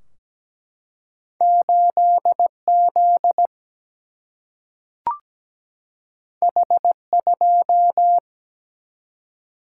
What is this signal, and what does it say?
Morse code, heard as a keyed tone, about 17 wpm, 700 Hz. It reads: 8Z H2
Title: Single-Letter Single-Number 17wpm